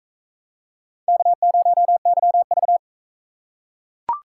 Morse code, heard as a keyed tone, about 42 wpm, 700 Hz. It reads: K0YV